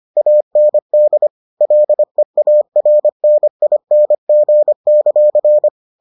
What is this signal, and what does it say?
Morse code, heard as a keyed tone, about 25 wpm, 600 Hz. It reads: AND LEARNING;